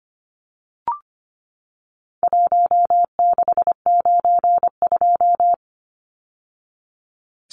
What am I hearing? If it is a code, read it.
Morse code, heard as a keyed tone, about 25 wpm, 700 Hz. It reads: 1692
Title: Numbers - 4-Digit 25wpm